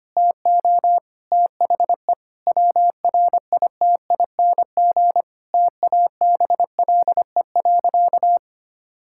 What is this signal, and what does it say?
Morse code, heard as a keyed tone, about 25 wpm, 700 Hz. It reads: TO THE WRITING TABLE.
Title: Sentences from Top 1000 Words - Review Part 5 - 25wpm